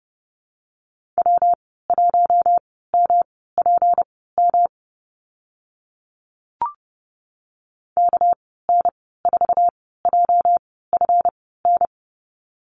Morse code, heard as a keyed tone, about 28 wpm, spaced as for 17 wpm, 700 Hz. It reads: W1MPM KN4JFN
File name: Lesson 37g - Letter x - Call Signs 30wpm